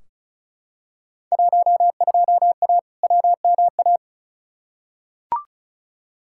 Morse code, heard as a keyed tone, about 36 wpm, 700 Hz. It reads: 12A WMA
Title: Field Day Exchanges Encore - 35wpm